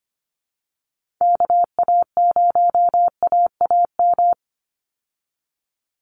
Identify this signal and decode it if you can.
Morse code, heard as a keyed tone, about 25 wpm, 700 Hz. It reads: KA0AAM
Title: Call Signs Repeated 2x - 25wpm